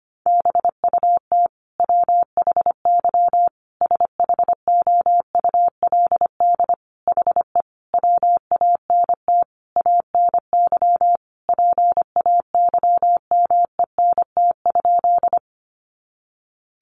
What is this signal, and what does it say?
Morse code, heard as a keyed tone, about 25 wpm, 700 Hz. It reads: BUT WHY SHOULD HE WANT ANY PAYMENT?
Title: Sentences from Top 1200 Words 25wpm